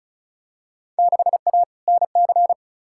Morse code, heard as a keyed tone, about 36 wpm, 690 Hz. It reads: 6A NC
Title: Field Day Exchanges Encore - 35wpm